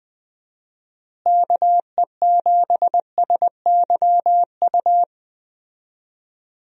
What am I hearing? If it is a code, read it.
Morse code, heard as a keyed tone, about 21 wpm, 700 Hz. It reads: KE7SYU
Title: USA Call Signs - 6 Characters - Repeated 2x - 20wpm